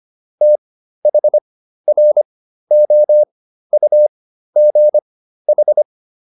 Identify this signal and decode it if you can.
Morse code, heard as a keyed tone, about 25 wpm, spaced as for 13 wpm, 600 Hz. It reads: THROUGH
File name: SS211 13